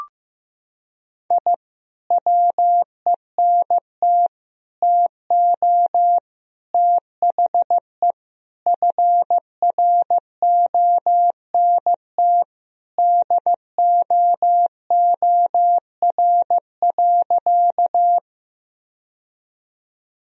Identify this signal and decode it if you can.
Morse code, heard as a keyed tone, about 15 wpm, 700 Hz. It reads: I WENT TO THE FRONT DOOR.
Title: Sentences from Top 1000 Words - Review Part 2 - 15wpm